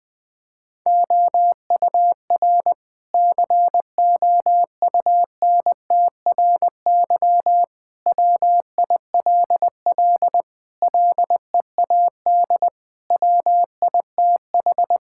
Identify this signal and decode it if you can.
Morse code, heard as a keyed tone, about 20 wpm, 700 Hz. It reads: OUR COUNTRY WILL LEAD WITH
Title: Sentences from Top 1000 Words - Review Part 5 - 20wpm